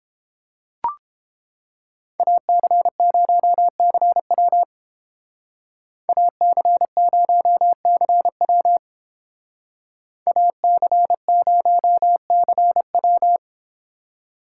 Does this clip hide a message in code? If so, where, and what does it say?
Yes: Morse code from 2.2 to 13.4 s, heard as a keyed tone, about 29 wpm, 700 Hz. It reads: AC0CW AC0CW AC0CW